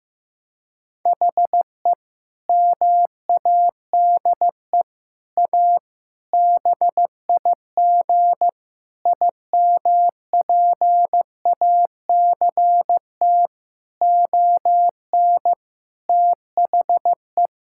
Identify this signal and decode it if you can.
Morse code, heard as a keyed tone, about 16 wpm, 700 Hz. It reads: HE MADE A BIG IMPACT ON THE
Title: Sentences from Top 1000 Words - Review Part 6 - 15wpm